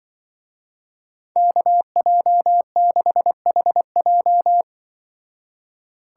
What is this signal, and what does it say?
Morse code, heard as a keyed tone, about 23 wpm, 700 Hz. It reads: KJ6HJ
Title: Speed Racing - USA 2x2 Call Signs 20wpm